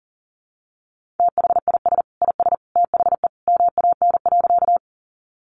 Morse code, heard as a keyed tone, about 40 wpm, 700 Hz. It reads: THIS IS THE MAN.
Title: Sentences from Top 200 Words 40wpm